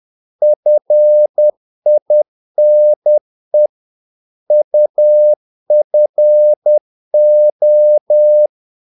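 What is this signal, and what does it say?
Morse code, heard as a keyed tone, about 10 wpm, 600 Hz. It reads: FINE UFO
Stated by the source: Morse practice of short Danish words at 10 wpm, from CW med Gnister.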